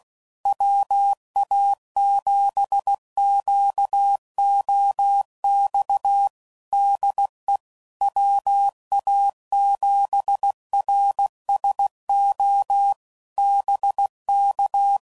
Morse code, heard as a keyed tone, about 16 wpm, 800 Hz. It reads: WA7QOX DE WA7RSO BK
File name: CW Code Groups @15WPM / QSOs / WA7QOX